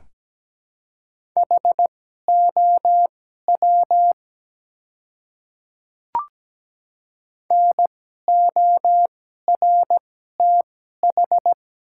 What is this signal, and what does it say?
Morse code, heard as a keyed tone, about 17 wpm, spaced as for 12 wpm, 700 Hz. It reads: HOW NORTH
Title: 2x Character Spacing - Top 300 Words - 17wpm